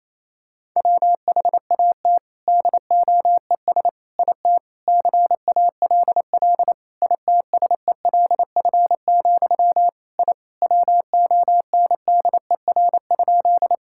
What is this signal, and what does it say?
Morse code, heard as a keyed tone, about 28 wpm, 700 Hz. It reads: WHAT DOES IT CALL ITSELF, I WONDER?
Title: Sentences from Top 1000 Words - Review Part 5 - 28wpm